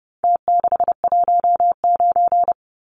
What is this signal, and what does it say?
Morse code, heard as a keyed tone, about 30 wpm, 700 Hz. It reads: T619
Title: Numbers - Contest Style - 4-Digit 30wpm